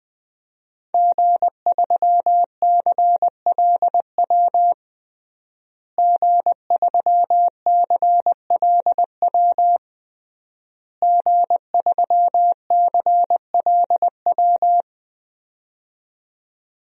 Morse code, heard as a keyed tone, about 21 wpm, 700 Hz. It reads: G3CLW G3CLW G3CLW
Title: DXCC-100 Call Signs - 5 Characters - Repeated 3x - 20wpm